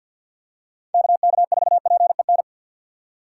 Morse code, heard as a keyed tone, about 52 wpm, 700 Hz. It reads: KK4PEN
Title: USA Call Signs - 6 Characters - 4x Character Spacing - 50wpm